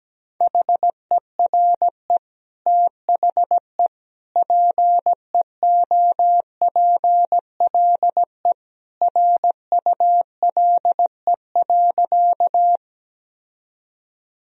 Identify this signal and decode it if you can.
Morse code, heard as a keyed tone, about 17 wpm, 700 Hz. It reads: HERE THE PEOPLE RULE.
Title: Sentences from Top 1000 Words - Review Part 2 - 17wpm